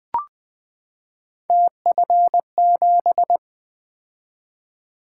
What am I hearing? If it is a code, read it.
Morse code, heard as a keyed tone, about 20 wpm, 700 Hz. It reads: TF7
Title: Three Letter-Number Combinations - 20wpm